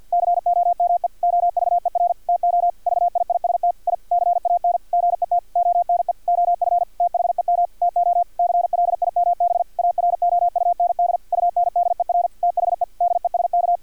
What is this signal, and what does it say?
Morse code, heard as a keyed tone, about 50 wpm, 700 Hz, with radio noise. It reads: COME OVER TO VIIST I CAN GET ONE OF THEM TO CLIMB AROUND UNDER THE DESK
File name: archived / CRSnet